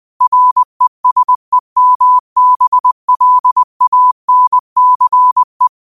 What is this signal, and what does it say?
Morse code, heard as a keyed tone, about 20 wpm, 1000 Hz. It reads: RESEMBLANCE